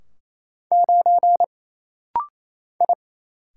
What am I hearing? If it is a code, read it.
Morse code, heard as a keyed tone, about 28 wpm, 700 Hz. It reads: 9 I
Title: Single Letter-Number - Rapid-Fire 28wpm